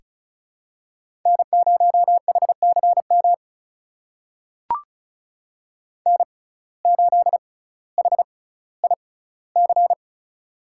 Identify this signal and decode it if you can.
Morse code, heard as a keyed tone, about 36 wpm, spaced as for 18 wpm, 700 Hz. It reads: N0HCM N8HIC
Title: USA Call Signs - 5 Characters - 6x Character Spacing - 35wpm